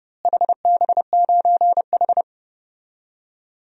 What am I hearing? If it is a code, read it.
Morse code, heard as a keyed tone, about 30 wpm, 700 Hz. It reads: HB9H
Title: DXCC-100 Call Signs - 4 Characters - Repeated 3x - 30wpm